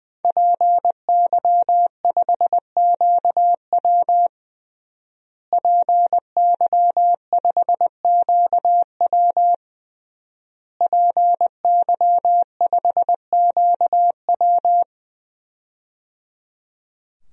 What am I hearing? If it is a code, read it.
Morse code, heard as a keyed tone, about 20 wpm, 700 Hz. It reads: PY5QW PY5QW PY5QW
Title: DXCC-100 Call Signs - 5 Characters - Repeated 3x - 20wpm